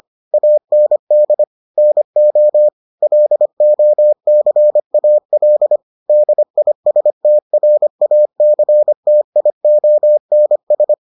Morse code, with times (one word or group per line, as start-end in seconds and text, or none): 0.3-1.4 AND
1.8-2.7 NO
3.0-5.8 LOCAL
6.1-10.9 DISTRACTIONS